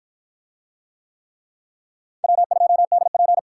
Morse code, heard as a keyed tone, about 53 wpm, 700 Hz. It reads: W2DP